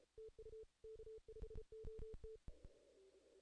Morse code, heard as a keyed tone, about 32 wpm, 440 Hz. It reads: TU K5OT
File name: 2016 JAN CW NAQP / 2016 Jan naqp k5ot - 0-11